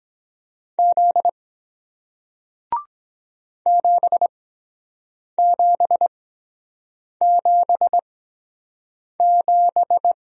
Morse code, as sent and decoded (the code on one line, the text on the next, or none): --.. / --... --... --... --...
Z 7777